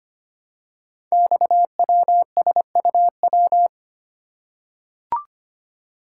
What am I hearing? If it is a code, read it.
Morse code, heard as a keyed tone, about 25 wpm, 700 Hz. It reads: XWSUW
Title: Code Groups - 5 Letters - 25wpm